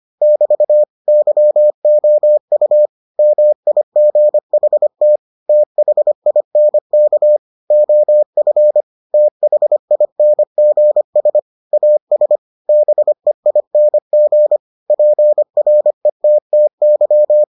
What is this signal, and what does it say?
Morse code, heard as a keyed tone, about 25 wpm, 600 Hz. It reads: <BT> YOU MIGHT THINK OF THINGS AS BEING PRETTY